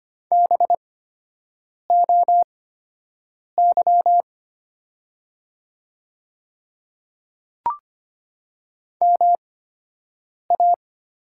Morse code, heard as a keyed tone, about 25 wpm, spaced as for 7 wpm, 700 Hz. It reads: BOY MA
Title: 8x Character Spacing - Top 200 Words - 25wpm